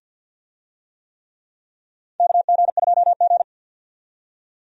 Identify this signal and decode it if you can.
Morse code, heard as a keyed tone, about 52 wpm, 700 Hz. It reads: KG2G
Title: USA Call Signs - 4 Characters - 50wpm